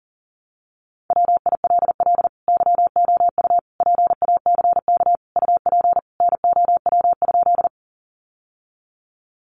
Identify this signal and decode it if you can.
Morse code, heard as a keyed tone, about 40 wpm, 700 Hz. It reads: WILL YOU PACK UP NOW?
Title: Sentences from Top 1300 Words 40wpm